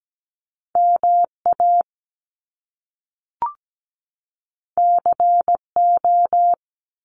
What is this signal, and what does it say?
Morse code, heard as a keyed tone, about 17 wpm, 700 Hz. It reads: MA CO
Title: US States 17wpm